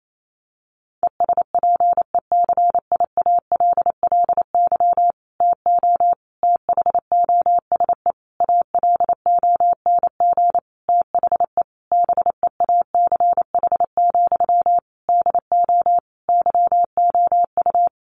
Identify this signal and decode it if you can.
Morse code, heard as a keyed tone, about 28 wpm, 700 Hz. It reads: ESPECIALLY TO THOSE ALONG THE BEACH, DO YOU
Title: Sentences from Top 1500 Words 28wpm